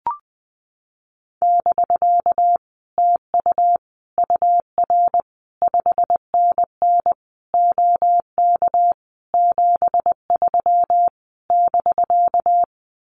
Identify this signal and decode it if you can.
Morse code, heard as a keyed tone, about 20 wpm, 700 Hz. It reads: <BK> TU UR 5NN OK 73 <BK>
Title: Common QSO Phrases Encore 20wpm